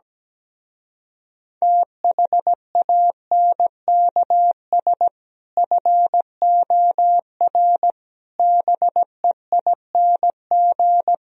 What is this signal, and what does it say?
Morse code, heard as a keyed tone, about 17 wpm, 700 Hz. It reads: THANKS FOR BEING
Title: Speed-Racing - Sets of 5 Words 17wpm